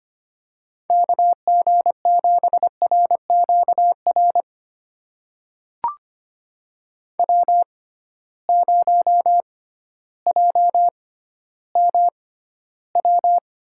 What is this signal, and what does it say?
Morse code, heard as a keyed tone, about 25 wpm, spaced as for 14 wpm, 700 Hz. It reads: KG7RQR W0JMW
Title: USA Call Signs - 6x Character Spacing - 25wpm